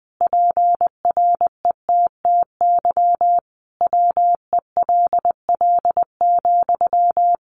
Morse code, heard as a keyed tone, about 20 wpm, 700 Hz. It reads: PRETTY WELL,
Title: Sentences from Top 1000 Words 20wpm